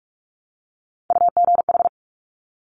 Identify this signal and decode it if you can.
Morse code, heard as a keyed tone, about 46 wpm, 700 Hz. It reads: UGH
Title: Top Three Letter Combinations in Words 45wpm